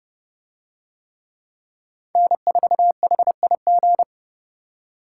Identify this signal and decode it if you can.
Morse code, heard as a keyed tone, about 27 wpm, 700 Hz. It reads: N4HIG